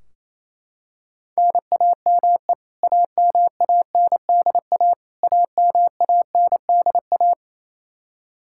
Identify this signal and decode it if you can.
Morse code, heard as a keyed tone, about 28 wpm, 700 Hz. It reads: NAME AMANDA AMANDA
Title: K1USN SST QSO Practice 28wpm